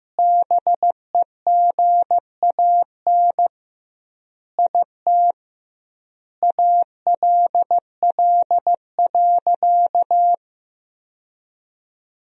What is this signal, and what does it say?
Morse code, heard as a keyed tone, about 15 wpm, 700 Hz. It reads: BEGAN IT ALL.
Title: 2x Word-Spacing - Sentences from Top 300 Words - 15wpm